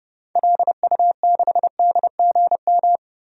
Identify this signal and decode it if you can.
Morse code, heard as a keyed tone, about 29 wpm, 700 Hz. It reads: LU6DGM
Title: DXCC-100 Call Signs - 6 Characters - 30wpm